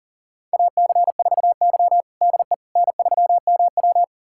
Morse code, heard as a keyed tone, about 40 wpm, 700 Hz. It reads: AC4Y DE N3MW
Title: USA Callsign DE Callsign - 4 Characters - 40wpm